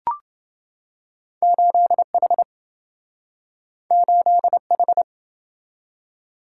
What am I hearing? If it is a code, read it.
Morse code, heard as a keyed tone, about 27 wpm, 700 Hz. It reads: 8H 8H